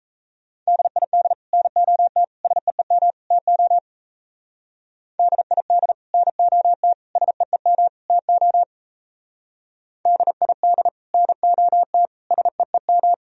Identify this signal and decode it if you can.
Morse code, heard as a keyed tone, about 33 wpm, 700 Hz. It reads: DID NOT SEEM TO DID NOT SEEM TO DID NOT SEEM
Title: Speed-Racing - Sets of 4 Words 28wpm